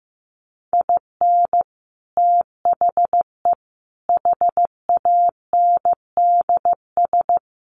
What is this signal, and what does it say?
Morse code, heard as a keyed tone, about 15 wpm, 700 Hz. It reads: IN THE HANDS
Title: Sets of 4 Words 15wpm